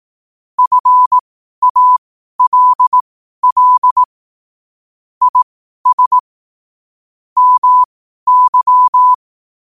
Morse code, heard as a keyed tone, about 18 wpm, spaced as for 12 wpm, 1000 Hz. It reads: FALL IS MY